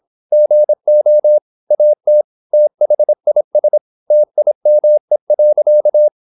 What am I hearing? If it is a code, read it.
Morse code, heard as a keyed tone, about 26 wpm, 600 Hz. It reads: GO AT THIS TIME.
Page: Never 26wpm | CW med Gnister